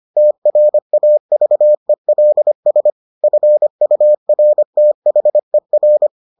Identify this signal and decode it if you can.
Morse code, heard as a keyed tone, about 25 wpm, 600 Hz. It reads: TRAVELS FURTHER